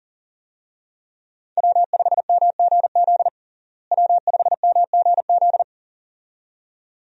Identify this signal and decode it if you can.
Morse code, heard as a keyed tone, about 40 wpm, 700 Hz. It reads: W5MGZ W5MGZ
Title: USA Call Signs - 5 Characters - Repeated 2x - 40wpm